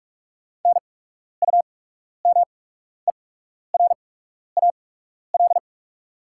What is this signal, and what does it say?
Morse code, heard as a keyed tone, about 44 wpm, spaced as for 12 wpm, 690 Hz. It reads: NUMERAL
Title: 8x Character Spacing - Top 500 Words - 45wpm